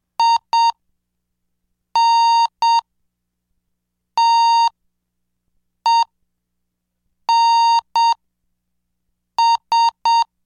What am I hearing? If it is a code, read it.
Morse code, heard as a keyed tone, about 7 wpm, 920 Hz. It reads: I N T E N S